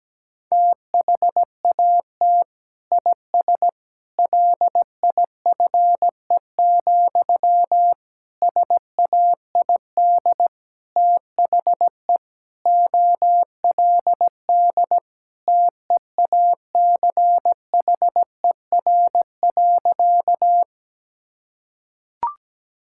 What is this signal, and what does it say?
Morse code, heard as a keyed tone, about 17 wpm, 700 Hz. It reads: THAT IS LIFE, SAID THE OLD TEACHER.
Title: Sentences from Top 1000 Words - Review Part 6 - 17wpm